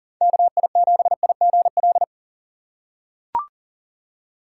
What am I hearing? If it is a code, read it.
Morse code, heard as a keyed tone, about 40 wpm, 700 Hz. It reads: KI7IGL
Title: USA Call Signs - 6 Characters - 4x Character Spacing - 40wpm